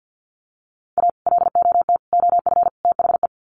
Morse code, heard as a keyed tone, about 50 wpm, 700 Hz. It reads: A LOT OF THE
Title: Sets of 4 Words 50wpm